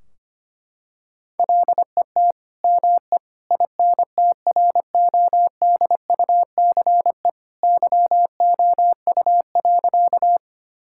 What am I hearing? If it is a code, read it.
Morse code, heard as a keyed tone, about 25 wpm, 700 Hz. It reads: LET ME INTRODUCE YOU.
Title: Sentences from Top 1600 Words 25wpm